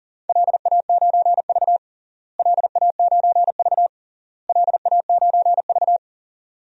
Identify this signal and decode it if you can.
Morse code, heard as a keyed tone, about 40 wpm, 700 Hz. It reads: LA9V LA9V LA9V
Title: DXCC-100 Call Signs - 4 Characters - Repeated 3x - 40wpm